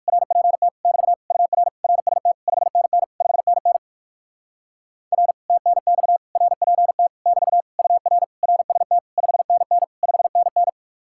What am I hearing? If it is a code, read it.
Morse code, heard as a keyed tone, about 59 wpm, 700 Hz. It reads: RPT <BT> UR RST 5NN 5NN R TNX RPT <BT> UR RST 5NN 5NN